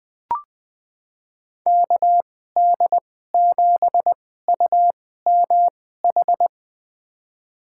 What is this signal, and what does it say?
Morse code, heard as a keyed tone, about 20 wpm, spaced as for 14 wpm, 700 Hz. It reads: KD7UMH
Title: USA Call Signs - 6 Characters - 2x Character Spacing - 20wpm